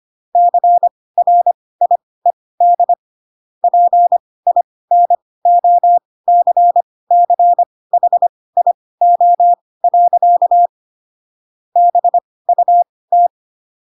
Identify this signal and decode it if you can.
Morse code, heard as a keyed tone, about 25 wpm, spaced as for 18 wpm, 700 Hz. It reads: CRIED PINOCCHIO. BUT
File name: Pinocchio / Pinocchio ch4 12 25 ik1hge